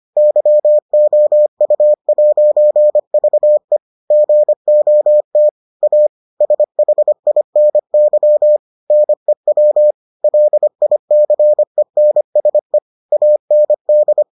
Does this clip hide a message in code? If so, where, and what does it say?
Yes: Morse code from 0.2 to 14.2 s, heard as a keyed tone, about 25 wpm, 600 Hz. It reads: YOU'VE GOT A SHINY NEW LICENSE AND